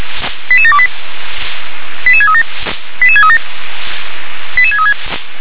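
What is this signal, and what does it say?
Five-tone selective call (ZVEI), heard as quick runs of five tones, tones of about 70 ms. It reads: 88728 88637 88637 88636
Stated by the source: ZVEI five-tone call received by SDR.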